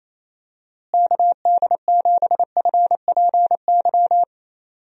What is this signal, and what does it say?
Morse code, heard as a keyed tone, about 27 wpm, 700 Hz. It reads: KD7FPY
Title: USA Call Signs - 6 Characters - Repeated 3x - 28wpm